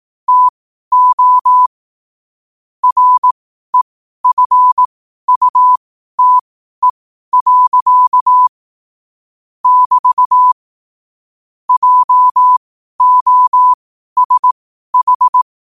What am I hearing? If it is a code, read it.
Morse code, heard as a keyed tone, about 18 wpm, spaced as for 12 wpm, 1000 Hz. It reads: TO REFUTE. <BT> JOSH